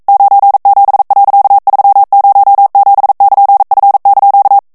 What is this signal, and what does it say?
Morse code, heard as a keyed tone, about 42 wpm, 800 Hz. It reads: 97.307<KN>F)